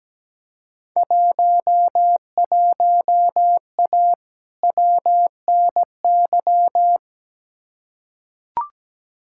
Morse code, heard as a keyed tone, about 17 wpm, 700 Hz. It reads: 11A WNY
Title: Field Day Exchanges Encore - 17wpm